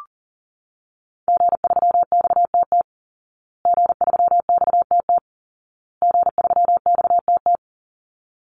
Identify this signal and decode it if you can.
Morse code, heard as a keyed tone, about 40 wpm, 700 Hz. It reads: G3XTT G3XTT G3XTT
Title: Call Signs Repeated 2x - 40wpm